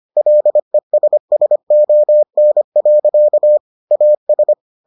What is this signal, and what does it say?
Morse code, heard as a keyed tone, about 25 wpm, 600 Hz. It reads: LESSON. AS